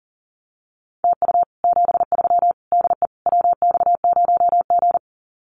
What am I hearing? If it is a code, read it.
Morse code, heard as a keyed tone, about 40 wpm, 700 Hz. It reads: TU 73 DE WX0G
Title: Common QSO Phrases Encore 40wpm